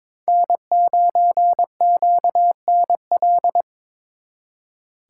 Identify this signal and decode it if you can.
Morse code, heard as a keyed tone, about 26 wpm, 700 Hz. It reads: N9QNL
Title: Speed Racing - USA Call Signs 22wpm